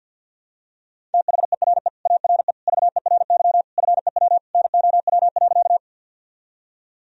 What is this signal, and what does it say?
Morse code, heard as a keyed tone, about 52 wpm, 700 Hz. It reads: THERE ARE VERY FEW NOW.
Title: Sentences from Top 1000 Words - Review Part 1 - 50wpm